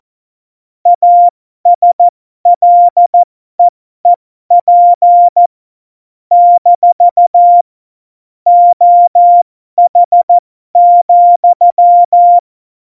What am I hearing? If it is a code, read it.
Morse code, heard as a keyed tone, about 14 wpm, spaced as for 12 wpm, 700 Hz. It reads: ASLEEP - OH,